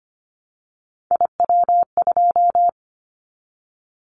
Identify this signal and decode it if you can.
Morse code, heard as a keyed tone, about 25 wpm, 700 Hz. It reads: IW2
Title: Callsign Prefixes for Top 25 DXCC - 25wpm